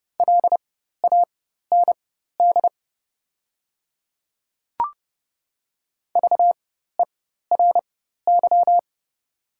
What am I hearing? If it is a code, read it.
Morse code, heard as a keyed tone, about 30 wpm, spaced as for 14 wpm, 700 Hz. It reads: LAND VERY
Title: 4x Character Spacing - Top 200 Words - 30wpm